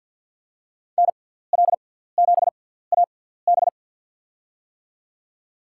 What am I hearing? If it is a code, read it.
Morse code, heard as a keyed tone, about 48 wpm, 700 Hz. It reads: N L 7 A B
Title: USA Call Signs - 5 Characters - 6x Character Spacing - 50wpm